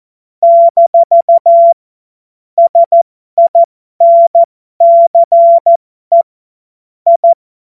Morse code, heard as a keyed tone, about 14 wpm, spaced as for 12 wpm, 680 Hz. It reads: - SINCE I
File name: Morsekurs / Witze